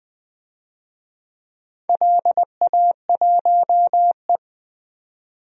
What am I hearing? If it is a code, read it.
Morse code, heard as a keyed tone, about 20 wpm, 700 Hz. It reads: LA1E
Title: DXCC-100 Call Signs - 4 Characters - 20wpm